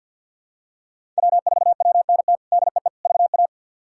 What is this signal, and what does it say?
Morse code, heard as a keyed tone, about 51 wpm, 700 Hz. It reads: W4WNT DEE VA
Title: K1USN SST QSO Practice - 50wpm